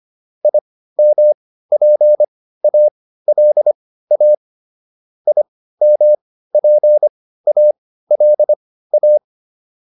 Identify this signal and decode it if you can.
Morse code, heard as a keyed tone, about 25 wpm, spaced as for 15 wpm, 600 Hz. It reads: IMPALA IMPALA
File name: IM6 15